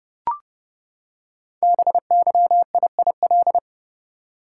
Morse code, heard as a keyed tone, about 30 wpm, 700 Hz. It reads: BYIIL
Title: Code Groups - 5 Letters - 30wpm